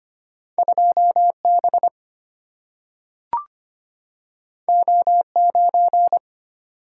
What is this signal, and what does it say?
Morse code, heard as a keyed tone, about 25 wpm, 700 Hz. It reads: 2B O9
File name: Two Letter-Number Combinations - 25wpm